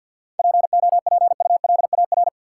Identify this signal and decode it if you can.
Morse code, heard as a keyed tone, about 52 wpm, 700 Hz. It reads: POPULAR
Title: Sets of 5 Words - New Word Drill - 50wpm